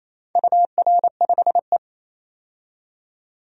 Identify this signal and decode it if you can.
Morse code, heard as a keyed tone, about 27 wpm, 690 Hz. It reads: UR5E
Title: DXCC-100 Call Signs - 4 Characters - 28wpm